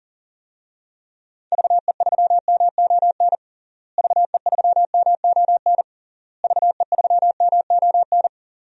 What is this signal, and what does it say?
Morse code, heard as a keyed tone, about 40 wpm, 700 Hz. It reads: VE3MON VE3MON VE3MON
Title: DXCC-100 Call Signs - 6 Characters - Repeated 3x - 40wpm